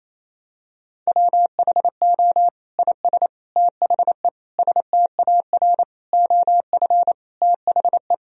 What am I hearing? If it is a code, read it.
Morse code, heard as a keyed tone, about 28 wpm, 700 Hz. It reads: WHO IS THE STAR OF THE
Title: Sentences from Top 1000 Words - Review Part 2 - 28wpm